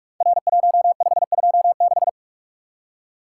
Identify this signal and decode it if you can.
Morse code, heard as a keyed tone, about 44 wpm, 700 Hz. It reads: R1526
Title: Code Groups - 5 Letter:Numbers - 45wpm